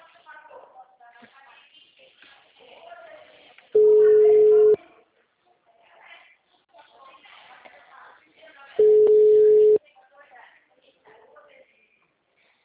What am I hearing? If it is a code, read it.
Ringback tone (Europe, 425 Hz), with speech.
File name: out- -2021- - - .0